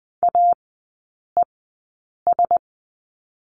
Morse code, heard as a keyed tone, about 20 wpm, spaced as for 8 wpm, 700 Hz. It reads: AES
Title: 041 Lesson 8e - Number 1 - Three Characters 20wpm